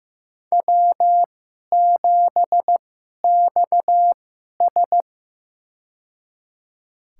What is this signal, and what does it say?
Morse code, heard as a keyed tone, about 16 wpm, spaced as for 11 wpm, 700 Hz. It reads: W7XS
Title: USA Call Signs - 4 Characters - 2x Character Spacing - 15wpm